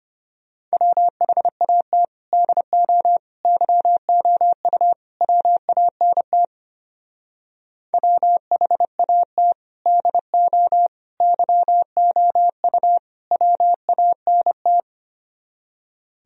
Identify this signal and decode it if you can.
Morse code, heard as a keyed tone, about 32 wpm, 700 Hz. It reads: WHAT DO YOU WANT WHAT DO YOU WANT